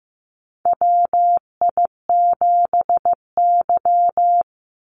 Morse code, heard as a keyed tone, about 15 wpm, 700 Hz. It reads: WI7Y
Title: USA Call Signs 15wpm